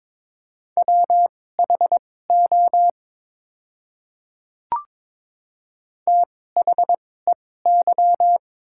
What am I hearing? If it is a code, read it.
Morse code, heard as a keyed tone, about 22 wpm, spaced as for 16 wpm, 700 Hz. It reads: WHO THEY
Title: 2x Character Spacing - Top 100 Words - 22wpm